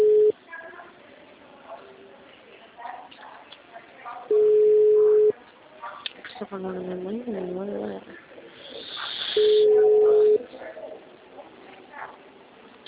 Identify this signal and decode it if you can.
Ringback tone (Europe, 425 Hz), with speech.
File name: out- -2003- - - .74